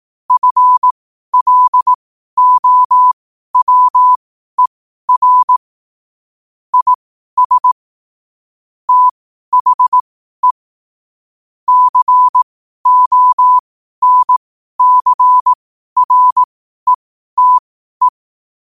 Morse code, heard as a keyed tone, about 18 wpm, spaced as for 12 wpm, 1000 Hz. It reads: FLOWER IS THE CONCRETE
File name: qotdfiles / 12wpm